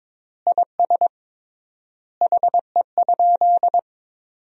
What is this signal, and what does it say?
Morse code, heard as a keyed tone, about 22 wpm, 700 Hz. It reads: IS HE?
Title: 3x Word-Spacing - Sentences from Top 500 Words - 22wpm